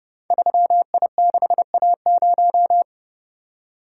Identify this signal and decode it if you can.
Morse code, heard as a keyed tone, about 30 wpm, 700 Hz. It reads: 3I6A0
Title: Code Groups - 5 Letter:Numbers - 30wpm